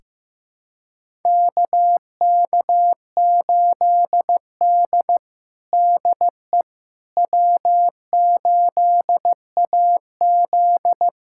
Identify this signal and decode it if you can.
Morse code, heard as a keyed tone, about 16 wpm, 700 Hz. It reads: KK8D DE W8AZ
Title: USA Callsign DE Callsign - 4 Characters - 15wpm